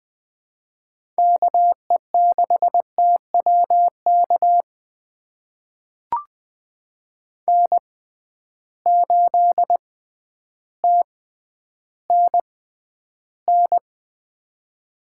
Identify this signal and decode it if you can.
Morse code, heard as a keyed tone, about 20 wpm, spaced as for 11 wpm, 700 Hz. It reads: KE6TWK N8TNN
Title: USA Call Signs - 6x Character Spacing - 20wpm